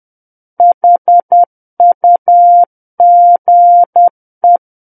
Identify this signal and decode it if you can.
Morse code, heard as a keyed tone, about 10 wpm, 700 Hz. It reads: HUGE
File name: Book One / 01-Chapter 1 - 10 WPM